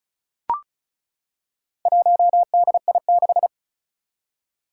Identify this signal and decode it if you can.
Morse code, heard as a keyed tone, about 36 wpm, 700 Hz. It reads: 1DI6
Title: Four Challenging Letter-Number Combinations 35wpm